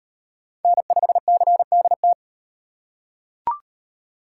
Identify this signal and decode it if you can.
Morse code, heard as a keyed tone, about 38 wpm, 700 Hz. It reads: N5CDT